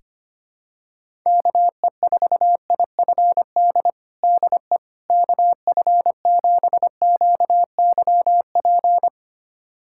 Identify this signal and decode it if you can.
Morse code, heard as a keyed tone, about 25 wpm, 700 Hz. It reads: KE4IFD DE KF7QYP
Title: USA Callsign DE Callsign - 6 Characters - 25wpm